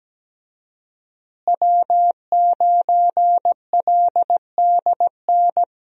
Morse code, heard as a keyed tone, about 17 wpm, 700 Hz. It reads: W9LDN
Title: USA Call Signs - 5 Characters - Repeated 3x - 17wpm